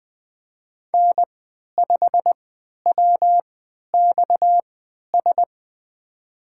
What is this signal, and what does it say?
Morse code, heard as a keyed tone, about 20 wpm, spaced as for 11 wpm, 700 Hz. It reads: N5WXS
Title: USA Call Signs - 5 Characters - 3x Character Spacing - 20wpm